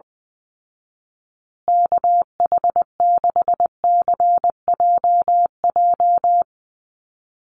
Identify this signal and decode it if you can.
Morse code, heard as a keyed tone, about 20 wpm, 700 Hz. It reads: KH6CJJ
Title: Call Signs Repeated 2x - 20wpm